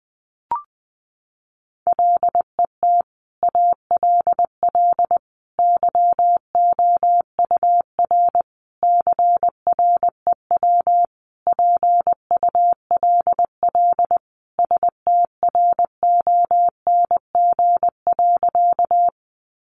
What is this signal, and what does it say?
Morse code, heard as a keyed tone, about 20 wpm, 700 Hz. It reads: LET ALL YOUR CREW PULL STRONG.
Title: Sentences from Top 1500 Words 20wpm